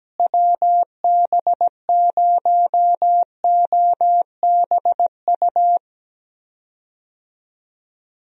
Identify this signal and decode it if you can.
Morse code, heard as a keyed tone, about 17 wpm, 700 Hz. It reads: WB0OBU
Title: USA Call Signs - 6 Characters - Repeated 3x - 17wpm